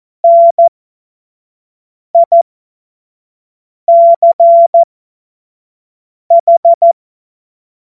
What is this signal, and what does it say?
Morse code, heard as a keyed tone, about 14 wpm, spaced as for 5 wpm, 680 Hz. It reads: NICH
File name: Morsekurs / Witze / Witz 27. Juni 2023 25 F70